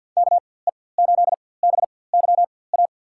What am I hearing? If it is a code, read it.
Morse code, heard as a keyed tone, about 50 wpm, spaced as for 23 wpm, 700 Hz. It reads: KE8BYA
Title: USA Call Signs - 6 Characters - 4x Character Spacing - 50wpm